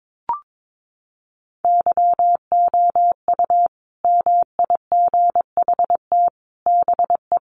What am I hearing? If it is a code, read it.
Morse code, heard as a keyed tone, about 22 wpm, 700 Hz. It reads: YOU MIGHT BE
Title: Sets of 5 Words Encore2 22wpm